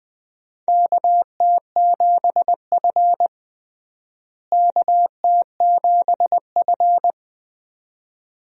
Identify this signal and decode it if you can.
Morse code, heard as a keyed tone, about 20 wpm, 700 Hz. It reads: KT7F KT7F
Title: USA Call Signs - 4 Characters - Repeated 2x - 20wpm